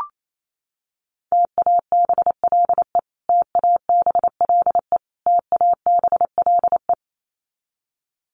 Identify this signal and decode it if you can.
Morse code, heard as a keyed tone, about 28 wpm, 700 Hz. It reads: TABLE TABLE TABLE
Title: Top 500 Words Repeated 28wpm